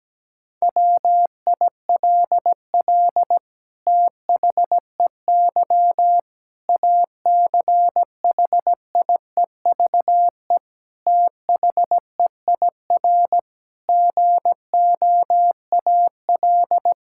Morse code, heard as a keyed tone, about 17 wpm, 700 Hz. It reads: WILL THEY ACHIEVE THEIR GOAL
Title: Sentences from Top 1000 Words - Review Part 6 - 17wpm